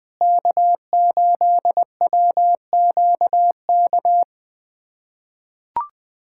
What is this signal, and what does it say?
Morse code, heard as a keyed tone, about 20 wpm, 700 Hz. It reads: K8WQK
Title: USA Callsign DE Callsign 20wpm